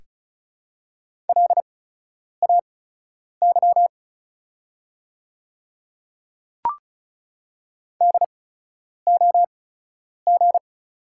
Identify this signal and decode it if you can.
Morse code, heard as a keyed tone, about 36 wpm, spaced as for 10 wpm, 700 Hz. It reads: LAY DOG
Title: 8x Character Spacing - Top 500 Words - 35wpm